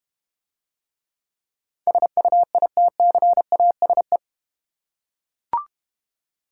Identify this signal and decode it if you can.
Morse code, heard as a keyed tone, about 32 wpm, 700 Hz. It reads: SUITCASE